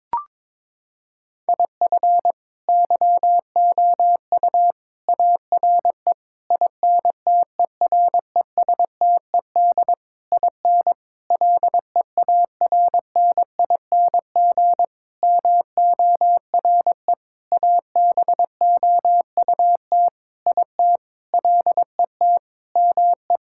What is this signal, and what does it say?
Morse code, heard as a keyed tone, about 22 wpm, 700 Hz. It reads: IF YOU ARE INTERESTED IN LEARNING MORE ABOUT IT LET ME
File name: practice / 22wpm / Sentences with 8-Word Phrases - 22wpm